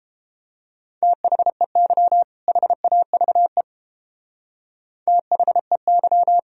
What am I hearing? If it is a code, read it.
Morse code, heard as a keyed tone, about 30 wpm, 700 Hz. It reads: THEY HAVE THEY